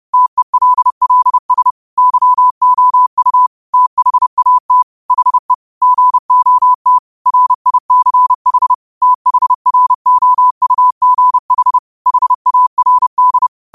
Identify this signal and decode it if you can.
Morse code, heard as a keyed tone, about 30 wpm, 1000 Hz. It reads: TELLS YOU THAT HE GOT RICH THROUGH HARD
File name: qotdfiles / 30wpm